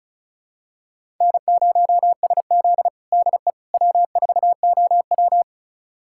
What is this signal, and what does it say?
Morse code, heard as a keyed tone, about 36 wpm, 700 Hz. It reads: N0SZ DE W4OW
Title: USA Callsign DE Callsign - 4 Characters - 35wpm